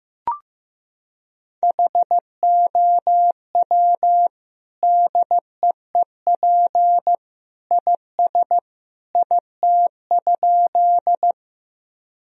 Morse code, heard as a keyed tone, about 15 wpm, 700 Hz. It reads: HOW DEEP IS IT?
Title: Sentences from Top 1000 Words - Review Part 3 - 15wpm